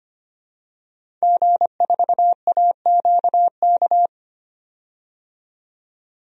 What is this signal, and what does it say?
Morse code, heard as a keyed tone, about 25 wpm, 700 Hz. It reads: G4AQK
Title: DXCC-100 Call Signs - 5 Characters - 25wpm